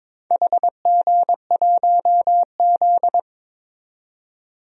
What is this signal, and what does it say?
Morse code, heard as a keyed tone, about 23 wpm, 700 Hz. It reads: HG1Z
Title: DXCC-100 Call Signs - 4 Characters - 22wpm